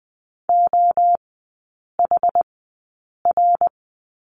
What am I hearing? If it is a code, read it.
Morse code, heard as a keyed tone, about 20 wpm, spaced as for 8 wpm, 700 Hz. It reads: OHR
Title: 100 Lesson 17e - Letter u - Three Characters 20wpm